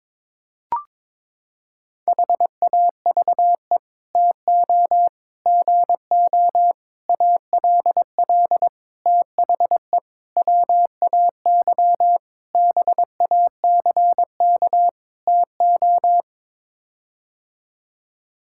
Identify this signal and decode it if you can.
Morse code, heard as a keyed tone, about 22 wpm, 700 Hz. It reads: HAVE TO GO ALL THE WAY BACK TO
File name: practice / 22wpm / Sets of 8 Words - 22wpm